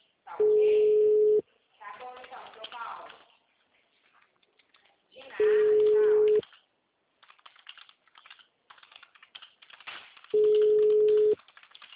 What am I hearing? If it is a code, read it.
Ringback tone (Europe, 425 Hz), with speech.
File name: out- -2025- - - .28